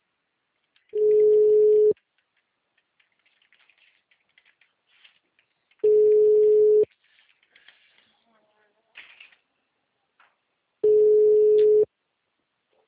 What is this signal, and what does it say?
Ringback tone (Europe, 425 Hz).
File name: out- -2006- - - .47414